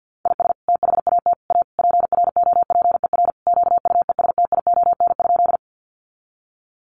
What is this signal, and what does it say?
Morse code, heard as a keyed tone, about 48 wpm, 700 Hz. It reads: IS THAT A PROPER QUESTION?
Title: Sentences from Top 1100 Words 50wpm